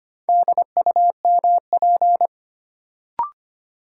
Morse code, heard as a keyed tone, about 25 wpm, 700 Hz. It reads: DUMP
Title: Sentences from Top 1300 Words - New Word Drill - 25wpm